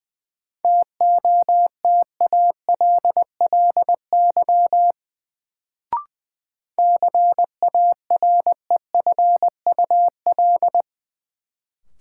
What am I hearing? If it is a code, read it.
Morse code, heard as a keyed tone, about 20 wpm, 700 Hz. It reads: TOTALLY CAREFUL
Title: Sentences from Top 1600 Words - New Word Drill Alt - 20wpm